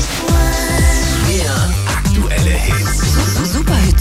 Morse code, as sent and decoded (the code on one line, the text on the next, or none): none